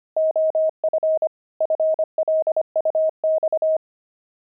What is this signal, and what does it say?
Morse code, heard as a keyed tone, about 25 wpm, 630 Hz. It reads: OF FLUX
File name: pangram-fs-25-cs-25-jelly